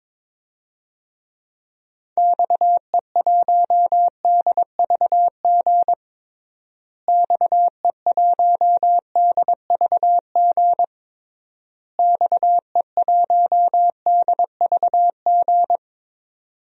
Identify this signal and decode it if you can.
Morse code, heard as a keyed tone, about 22 wpm, 700 Hz. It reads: XE1DVG XE1DVG XE1DVG
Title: DXCC-100 Call Signs - 6 Characters - Repeated 3x - 22wpm